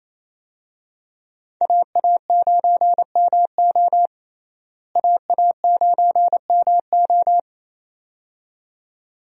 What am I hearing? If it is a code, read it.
Morse code, heard as a keyed tone, about 28 wpm, 700 Hz. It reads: AA9MO AA9MO
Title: USA Call Signs - 5 Characters - Repeated 2x - 28wpm